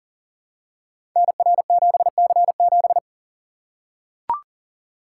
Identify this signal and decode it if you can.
Morse code, heard as a keyed tone, about 42 wpm, 700 Hz. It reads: NR7C7
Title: Code Groups - 5 Letter:Numbers - 40wpm